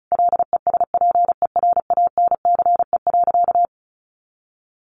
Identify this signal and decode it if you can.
Morse code, heard as a keyed tone, about 34 wpm, 700 Hz. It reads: LESPERANCE.
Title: A Sound Of Thunder 35wpm